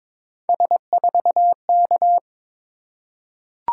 Morse code, heard as a keyed tone, about 22 wpm, 700 Hz. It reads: S4K
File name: practice / 22wpm / Three Letter-Number Combinations - 22wpm